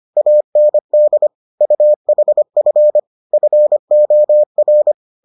Morse code, heard as a keyed tone, about 25 wpm, 600 Hz. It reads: AND UHF FOR